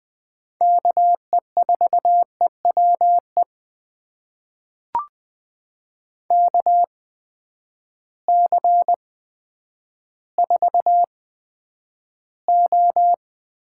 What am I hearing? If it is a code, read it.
Morse code, heard as a keyed tone, about 20 wpm, spaced as for 10 wpm, 700 Hz. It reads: KE4EWE KC4O
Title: USA Call Signs - 6 Characters - 8x Character Spacing - 20wpm